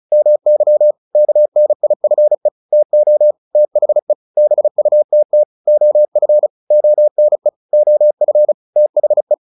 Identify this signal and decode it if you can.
Morse code, heard as a keyed tone, about 34 wpm, 600 Hz. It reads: MY KNIFE TO THE BUTT OF ONE OF THE
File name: the old man and the sea 2 35wpm